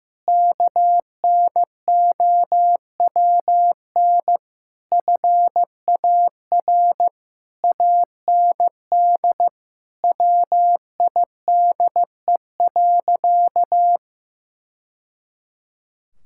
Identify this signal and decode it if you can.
Morse code, heard as a keyed tone, about 15 wpm, 700 Hz. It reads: KNOWN FAR AND WIDE.
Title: Sentences from Top 1000 Words - Review Part 6 - 15wpm